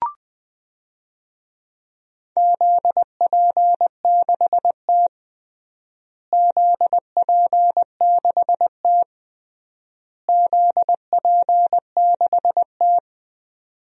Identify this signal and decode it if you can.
Morse code, heard as a keyed tone, about 20 wpm, 700 Hz. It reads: ZP6T ZP6T ZP6T
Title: DXCC-100 Call Signs - 4 Characters - Repeated 3x - 20wpm